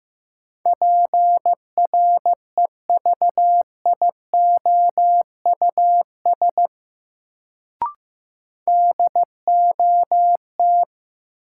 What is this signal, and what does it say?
Morse code, heard as a keyed tone, about 15 wpm, 700 Hz. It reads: PREVIOUS DOT
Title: Sentences from Top 1500 Words - New Word Drill - 15wpm